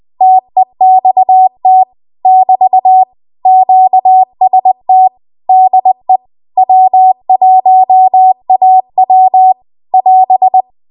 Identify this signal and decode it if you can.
Morse code, heard as a keyed tone, about 20 wpm, 750 Hz. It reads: TEXT <BT> QST DE W1AW <AS>
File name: Archive / 20 WPM / 20WPM